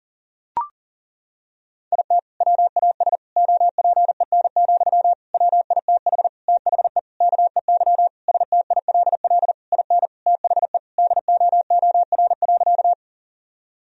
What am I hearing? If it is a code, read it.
Morse code, heard as a keyed tone, about 40 wpm, 700 Hz. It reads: IT WAS OPEN, WITH THE KEY STILL IN THE DOOR.
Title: Sentences from Top 1000 Words - Review Part 4 - 40wpm